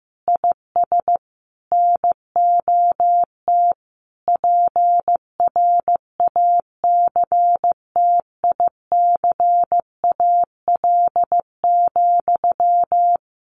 Morse code, heard as a keyed tone, about 15 wpm, 700 Hz. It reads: IS NOT PRACTICAL,
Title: Sentences from Top 1400 Words 15wpm